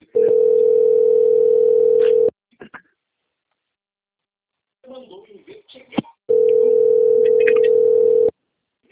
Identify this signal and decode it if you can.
Ringback tone (North America), 440 + 480 Hz, with speech.
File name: out- - - - - .36070